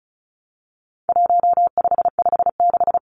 Morse code, heard as a keyed tone, about 34 wpm, 700 Hz. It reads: 1556
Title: Numbers - 4-Digit 35wpm